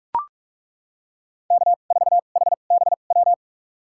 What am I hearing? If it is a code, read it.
Morse code, heard as a keyed tone, about 42 wpm, spaced as for 31 wpm, 700 Hz. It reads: K4HBW
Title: USA Call Signs - 5 Characters - 2x Character Spacing - 45wpm